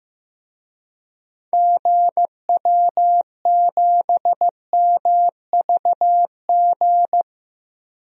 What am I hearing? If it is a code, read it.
Morse code, heard as a keyed tone, about 15 wpm, 700 Hz. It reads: GW7MVG
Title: DXCC-100 Call Signs - 6 Characters - 15wpm